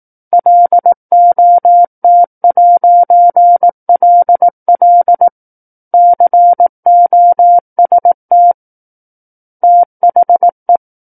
Morse code, heard as a keyed tone, about 18 wpm, 700 Hz. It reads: LOT'LL COST THE
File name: 18 WPM